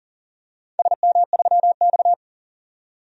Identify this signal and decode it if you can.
Morse code, heard as a keyed tone, about 37 wpm, 700 Hz. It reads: SM3X
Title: DXCC-100 Call Signs - 4 Characters - 40wpm